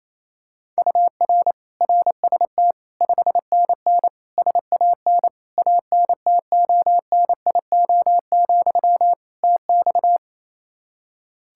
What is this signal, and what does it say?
Morse code, heard as a keyed tone, about 27 wpm, 700 Hz. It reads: UR RST 5NN SAN ANTONIO, TX
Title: Practice QSO - Exchange 1 - Beginning - 28wpm